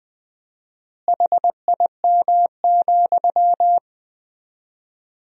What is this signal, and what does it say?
Morse code, heard as a keyed tone, about 20 wpm, 700 Hz. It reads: HIM,
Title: 4x Word-Spacing - Sentences from Top 400 Words - 20wpm